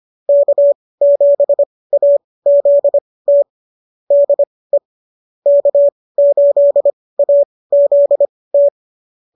Morse code, heard as a keyed tone, about 25 wpm, spaced as for 18 wpm, 590 Hz. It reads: K7AZT DE K8AZT